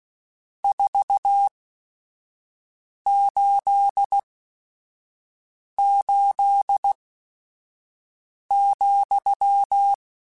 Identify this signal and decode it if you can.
Morse code, heard as a keyed tone, about 16 wpm, spaced as for 5 wpm, 800 Hz. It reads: 488,
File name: Your RSTs @5